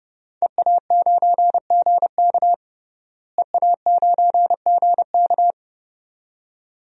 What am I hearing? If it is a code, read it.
Morse code, heard as a keyed tone, about 31 wpm, 700 Hz. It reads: EA9GK EA9GK
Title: DXCC-100 Call Signs - 5 Characters - Repeated 2x - 30wpm